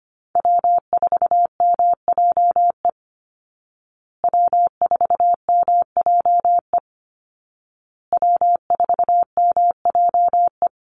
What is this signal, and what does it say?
Morse code, heard as a keyed tone, about 25 wpm, 700 Hz. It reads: W4MJE W4MJE W4MJE
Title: Call Signs Repeated 2x - 25wpm